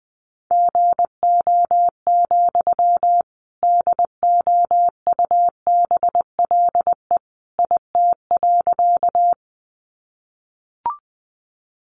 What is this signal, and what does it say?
Morse code, heard as a keyed tone, about 20 wpm, 700 Hz. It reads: GO, DOUBLE IT.
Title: Sentences from Top 1400 Words 20wpm